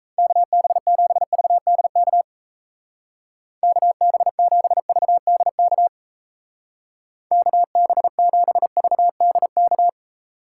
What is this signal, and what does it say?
Morse code, heard as a keyed tone, about 34 wpm, 700 Hz. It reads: KB7VDK KB7VDK KB7VDK
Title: Speed Racing - USA 2x3 Call Signs 28wpm